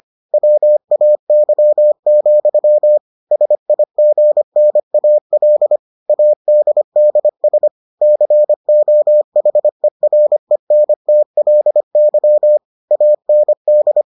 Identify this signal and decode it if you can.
Morse code, heard as a keyed tone, about 25 wpm, 600 Hz. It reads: WAY, SIGNAL ADDS COHERENTLY AND